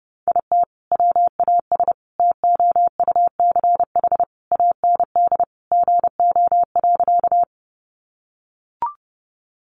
Morse code, heard as a keyed tone, about 30 wpm, 700 Hz. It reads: IT WAS TOUCH AND GO.
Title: Sentences from Top 1200 Words 30wpm